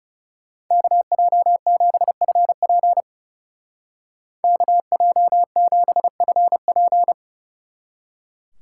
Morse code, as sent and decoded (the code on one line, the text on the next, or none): -.- .--- --... ..-. .--. / -.- .--- --... ..-. .--.
KJ7FP KJ7FP